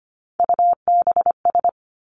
Morse code, heard as a keyed tone, about 25 wpm, 700 Hz. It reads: UBS
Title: Three Challenging Letter-Number Combinations 25wpm